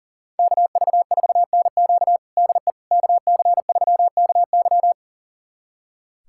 Morse code, heard as a keyed tone, about 37 wpm, 700 Hz. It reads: KV4NQ DE KC3KY
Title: USA Callsign DE Callsign 40wpm